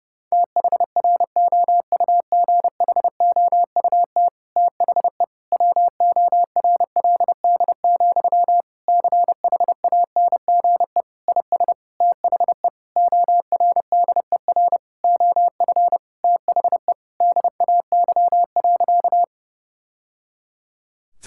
Morse code, heard as a keyed tone, about 30 wpm, 700 Hz. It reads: THROUGHOUT THE WORLD, CHANGE IS THE ORDER OF THE DAY.
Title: Sentences from Top 1000 Words - Review Part 5 - 30wpm